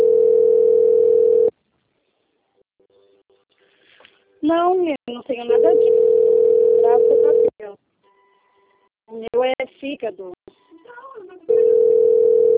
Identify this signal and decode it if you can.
Ringback tone (North America), 440 + 480 Hz, with speech.
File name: out- - - - - .19948